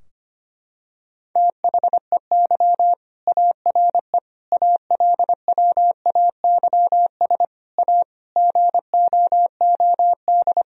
Morse code, heard as a keyed tone, about 25 wpm, 700 Hz. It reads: THEY ARE ALWAYS A GOOD
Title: Sentences from Top 1000 Words - Review Part 4 - 25wpm